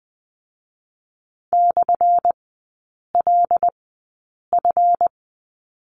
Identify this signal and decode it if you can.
Morse code, heard as a keyed tone, about 20 wpm, spaced as for 8 wpm, 690 Hz. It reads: /LF
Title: 212 Lesson 34e - Number 8 - Three Characters 20wpm